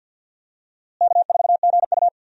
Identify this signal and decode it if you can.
Morse code, heard as a keyed tone, about 50 wpm, 700 Hz. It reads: K4GU
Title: USA Call Signs - 4 Characters - Repeated 3x - 50wpm